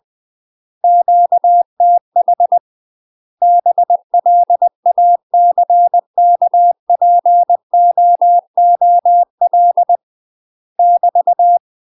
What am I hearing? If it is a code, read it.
Morse code, heard as a keyed tone, about 20 wpm, 700 Hz. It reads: QTH BLACKPOOL <BT>
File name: UK MASTER QSOs TRAINING / 20WPM / UK MASTER QSOs 0052